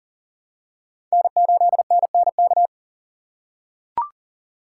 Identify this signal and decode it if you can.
Morse code, heard as a keyed tone, about 40 wpm, 700 Hz. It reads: N8NNK
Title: USA Call Signs - 5 Characters - 2x Character Spacing - 40wpm